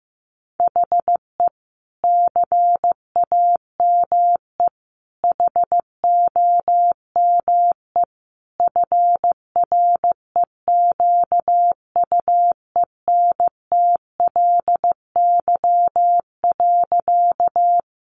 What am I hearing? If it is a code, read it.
Morse code, heard as a keyed tone, about 15 wpm, 700 Hz. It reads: HE CAME HOME FREQUENTLY.
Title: Sentences from Top 1300 Words 15wpm